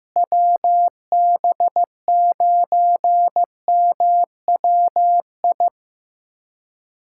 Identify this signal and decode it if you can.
Morse code, heard as a keyed tone, about 15 wpm, 700 Hz. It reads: WB9MWI
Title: USA Callsign DE Callsign 15wpm